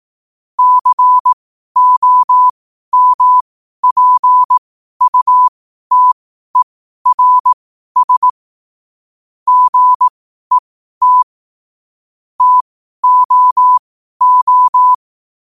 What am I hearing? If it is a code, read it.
Morse code, heard as a keyed tone, about 18 wpm, spaced as for 12 wpm, 1000 Hz. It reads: COMPUTERS GET TOO
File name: qotdfiles / 12wpm